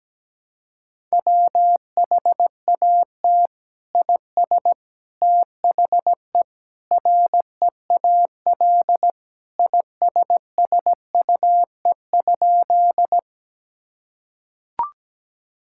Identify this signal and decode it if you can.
Morse code, heard as a keyed tone, about 17 wpm, 700 Hz. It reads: WHAT IS THE REAL ISSUE?
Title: Sentences from Top 1000 Words - Review Part 6 - 17wpm